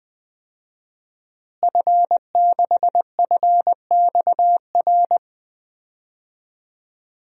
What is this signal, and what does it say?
Morse code, heard as a keyed tone, about 20 wpm, 700 Hz. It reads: F6FXR
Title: DXCC-100 Call Signs - 5 Characters - Repeated 2x - 20wpm